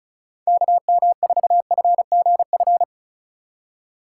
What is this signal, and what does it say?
Morse code, heard as a keyed tone, about 36 wpm, 700 Hz. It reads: KM4FGF
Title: USA Call Signs - 6 Characters - 4x Character Spacing - 35wpm